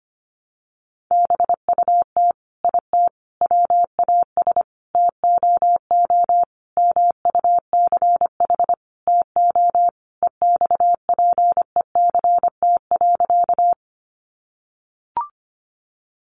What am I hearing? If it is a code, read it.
Morse code, heard as a keyed tone, about 25 wpm, 700 Hz. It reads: BUT IT WAS TOO MUCH TO EXPECT.
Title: Sentences from Top 1100 Words 25wpm